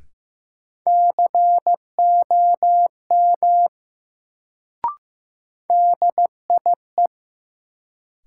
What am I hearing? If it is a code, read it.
Morse code, heard as a keyed tone, about 15 wpm, 700 Hz. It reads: COM DIE
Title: Sentences from Top 1300 Words - New Word Drill - 15wpm